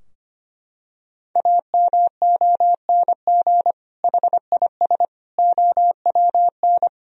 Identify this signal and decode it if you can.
Morse code, heard as a keyed tone, about 25 wpm, 700 Hz. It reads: AMONG HIS OWN
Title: Sentences from Top 1000 Words - Review Part 3 - 25wpm